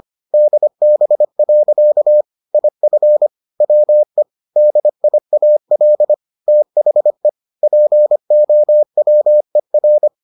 Morse code, heard as a keyed tone, about 25 wpm, 600 Hz. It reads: DB. IF WE DIAL THE POWER